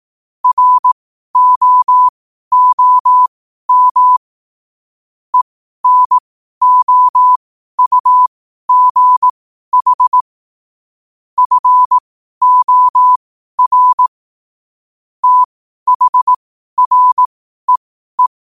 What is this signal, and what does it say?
Morse code, heard as a keyed tone, about 18 wpm, spaced as for 12 wpm, 1000 Hz. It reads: ROOM ENOUGH FOR THREE
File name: qotdfiles / 12wpm